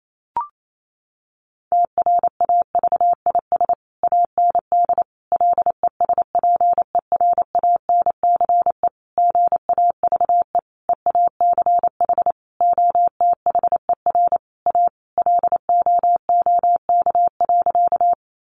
Morse code, heard as a keyed tone, about 28 wpm, 700 Hz. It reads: TRAVIS AND LESPERANCE GAVE EACH OTHER A LOOK.
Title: A Sound Of Thunder 28wpm